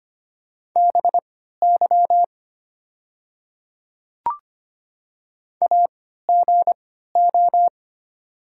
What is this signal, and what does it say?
Morse code, heard as a keyed tone, about 25 wpm, spaced as for 14 wpm, 700 Hz. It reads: BY AGO
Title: 3x Character Spacing - Top 500 Words - 25wpm